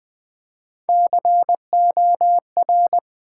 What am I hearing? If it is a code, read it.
Morse code, heard as a keyed tone, about 20 wpm, 700 Hz. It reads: COR
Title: Build A Word - 1 Letter At A Time - 4-Letter Words - 20wpm